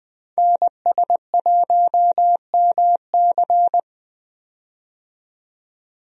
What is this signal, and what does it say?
Morse code, heard as a keyed tone, about 20 wpm, 700 Hz. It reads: NS1MC
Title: USA Call Signs - 5 Characters - 20wpm